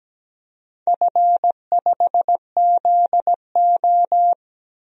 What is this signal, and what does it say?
Morse code, heard as a keyed tone, about 17 wpm, 700 Hz. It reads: F5ZO
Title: DXCC-100 Call Signs - 4 Characters - 17wpm